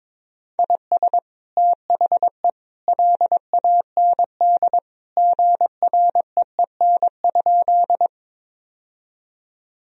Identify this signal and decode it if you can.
Morse code, heard as a keyed tone, about 22 wpm, 700 Hz. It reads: IS THE LAND GREEN?
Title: Sentences from Top 1000 Words - Review Part 2 - 22wpm